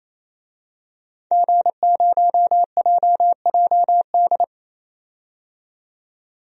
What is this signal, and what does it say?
Morse code, heard as a keyed tone, about 28 wpm, 700 Hz. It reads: G0JJD
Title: DXCC-100 Call Signs - 5 Characters - 28wpm